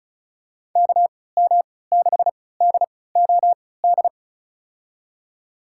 Morse code, heard as a keyed tone, about 34 wpm, spaced as for 20 wpm, 700 Hz. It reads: KM6DOD
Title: USA Call Signs - 6 Characters - 3x Character Spacing - 35wpm